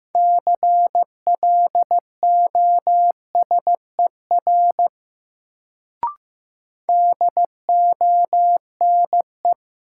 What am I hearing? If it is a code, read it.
Morse code, heard as a keyed tone, about 15 wpm, 700 Hz. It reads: CLOSER DONE
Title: Sets of 5 Words Encore - New Word Drill - 15wpm